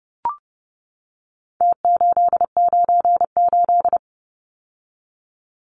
Morse code, heard as a keyed tone, about 30 wpm, 700 Hz. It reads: T898
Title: Numbers - Contest Style - 4-Digit 30wpm